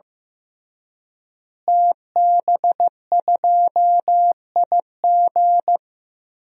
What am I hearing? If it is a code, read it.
Morse code, heard as a keyed tone, about 15 wpm, 700 Hz. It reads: TB2IG
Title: DXCC-100 Call Signs - 5 Characters - Repeated 3x - 15wpm